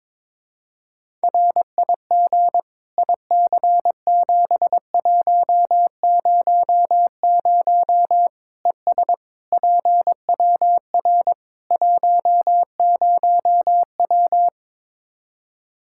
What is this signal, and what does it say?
Morse code, heard as a keyed tone, about 22 wpm, 700 Hz. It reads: RIG IC7100 ES PWR 10W
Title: K1USN SST QSO Practice 22wpm